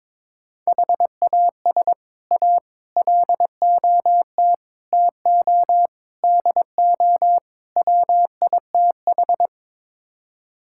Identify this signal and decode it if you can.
Morse code, heard as a keyed tone, about 21 wpm, 700 Hz. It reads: HAS A LOT TO DO WITH
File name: practice / 22wpm / Sets of 6 Words - 22wpm